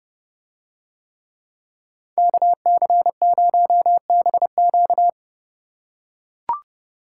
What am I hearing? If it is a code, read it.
Morse code, heard as a keyed tone, about 28 wpm, 700 Hz. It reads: KC0BQ